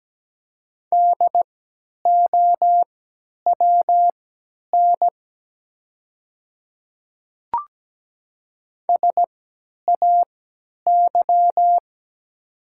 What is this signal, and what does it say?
Morse code, heard as a keyed tone, about 17 wpm, spaced as for 10 wpm, 700 Hz. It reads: DOWN SAY
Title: 3x Character Spacing - Top 300 Words - 17wpm